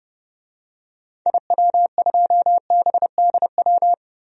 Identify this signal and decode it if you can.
Morse code, heard as a keyed tone, about 30 wpm, 700 Hz. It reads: IW2BDW
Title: DXCC-100 Call Signs - 6 Characters - Repeated 3x - 30wpm